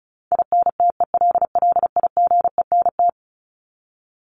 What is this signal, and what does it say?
Morse code, heard as a keyed tone, about 34 wpm, 700 Hz. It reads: INTELLIGENT
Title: Top 751-1000 Words 35wpm